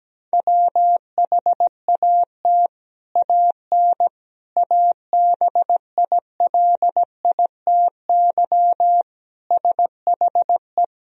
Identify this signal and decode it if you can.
Morse code, heard as a keyed tone, about 17 wpm, 700 Hz. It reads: WHAT AN ABILITY SHE
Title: Sentences from Top 1000 Words - Review Part 3 - 17wpm